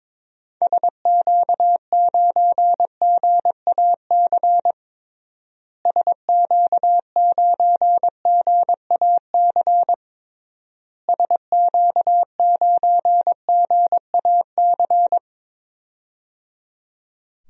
Morse code, heard as a keyed tone, about 22 wpm, 700 Hz. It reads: SQ9GAC SQ9GAC SQ9GAC
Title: DXCC-100 Call Signs - 6 Characters - Repeated 3x - 22wpm